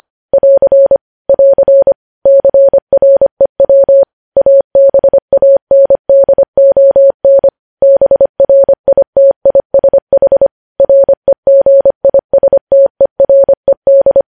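Morse code, heard as a keyed tone, about 25 wpm, 550 Hz. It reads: <AR> <AR> CREW ABANDON BRITISH REGISTERED
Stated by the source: Morse news headlines sent at 25 wpm.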